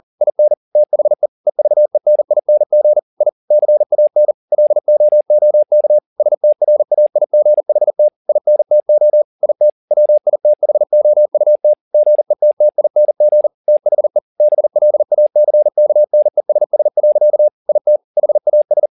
Morse code, heard as a keyed tone, about 40 wpm, 600 Hz. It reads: IN THE EVENING I CAN LOOK STRAIGHT INTO IT WITHOUT GETTING THE BLACKNESS. IT HAS